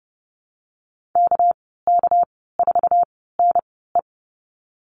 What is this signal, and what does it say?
Morse code, heard as a keyed tone, about 30 wpm, spaced as for 17 wpm, 700 Hz. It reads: KK4NE
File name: Lesson 37g - Letter x - Call Signs 30wpm